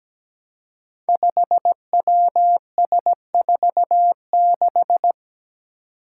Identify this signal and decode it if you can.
Morse code, heard as a keyed tone, about 17 wpm, 700 Hz. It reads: 5WS46
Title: Five Challenging Letter-Number Combinations 17wpm